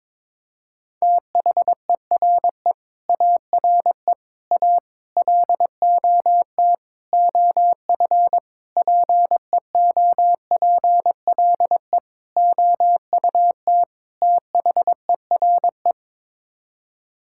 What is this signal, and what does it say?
Morse code, heard as a keyed tone, about 22 wpm, 700 Hz. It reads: THERE ARE A LOT OF PEOPLE OUT THERE
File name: practice / 22wpm / Sets of 8 Words - 22wpm